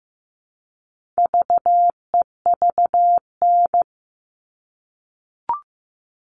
Morse code, heard as a keyed tone, about 15 wpm, 700 Hz. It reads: VEVN
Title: All Cut Numbers - 4-Digit 15wpm